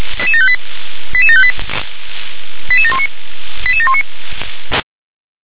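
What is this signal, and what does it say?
Five-tone selective call (ZVEI), heard as quick runs of five tones, tones of about 70 ms. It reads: 88758 88719 88719
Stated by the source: ZVEI five-tone call received by SDR.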